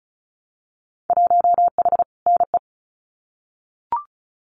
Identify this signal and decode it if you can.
Morse code, heard as a keyed tone, about 34 wpm, 700 Hz. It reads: 1H NE
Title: Winter Field Day Exchanges Encore 35wpm